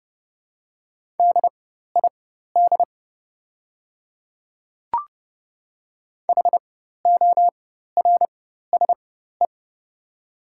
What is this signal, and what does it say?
Morse code, heard as a keyed tone, about 30 wpm, spaced as for 14 wpm, 700 Hz. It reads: DID HORSE
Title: 4x Character Spacing - Top 300 Words - 30wpm